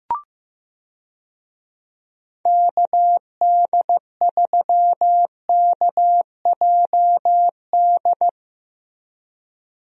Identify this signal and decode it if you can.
Morse code, heard as a keyed tone, about 15 wpm, 700 Hz. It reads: KD3KJD
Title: USA Call Signs - 6 Characters - Repeated 2x - 15wpm